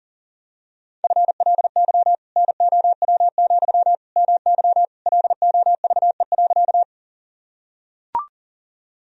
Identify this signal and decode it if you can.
Morse code, heard as a keyed tone, about 40 wpm, 700 Hz. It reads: FLY NOW, MY LOVE.
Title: Sentences from Top 1000 Words - Review Part 2 - 40wpm